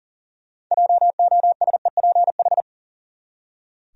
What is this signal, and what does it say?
Morse code, heard as a keyed tone, about 40 wpm, 700 Hz. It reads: JOSEPH
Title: USA Common Names - 40wpm